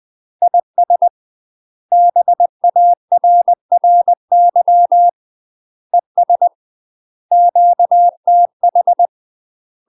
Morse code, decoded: IS BARRY ES QTH

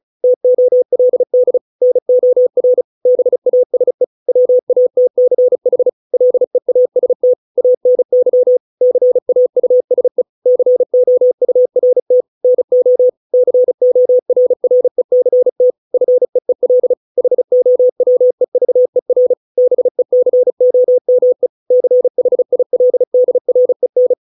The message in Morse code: TOLD NOR BASE WATCH LEAST ANY CAUSE COURT NO CORRECT FEEL HOWEVER BECOME CHILDREN

